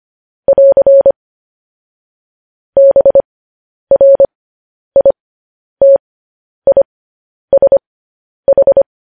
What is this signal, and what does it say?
Morse code, heard as a keyed tone, about 25 wpm, spaced as for 10 wpm, 550 Hz. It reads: <AR> BRITISH
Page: news-headlines-in-morse-code-at-10-wpm